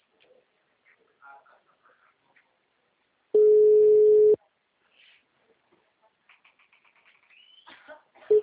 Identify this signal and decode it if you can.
Ringback tone (Europe, 425 Hz).